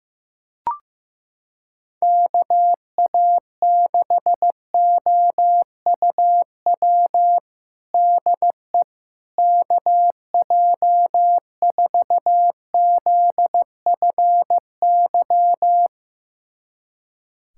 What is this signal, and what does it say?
Morse code, heard as a keyed tone, about 15 wpm, 700 Hz. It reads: KA6OUW DE KJ4ZFY
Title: USA Callsign DE Callsign - 6 Characters - 15wpm